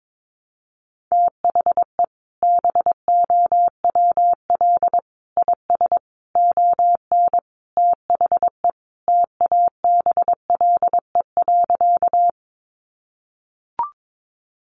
Morse code, heard as a keyed tone, about 22 wpm, 700 Hz. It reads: THE BOWL IS ON THE TABLE.
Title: Sentences from Top 1200 Words 22wpm